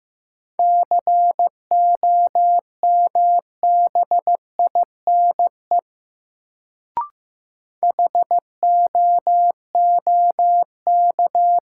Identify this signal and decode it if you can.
Morse code, heard as a keyed tone, about 15 wpm, 700 Hz. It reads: COMBINE HOOK
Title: Sentences from Top 1600 Words - New Word Drill Alt - 15wpm